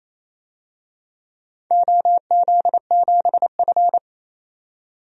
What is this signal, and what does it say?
Morse code, heard as a keyed tone, about 28 wpm, 700 Hz. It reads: OZ7F
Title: DXCC-100 Call Signs - 4 Characters - 28wpm